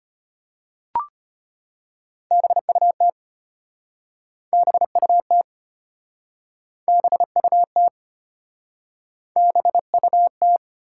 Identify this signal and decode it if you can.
Morse code, heard as a keyed tone, about 33 wpm, 700 Hz. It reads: BUT BUT BUT BUT